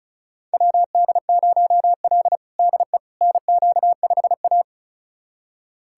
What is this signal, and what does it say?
Morse code, heard as a keyed tone, about 36 wpm, 700 Hz. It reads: WD0L DE NQ5A
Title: USA Callsign DE Callsign - 4 Characters - 35wpm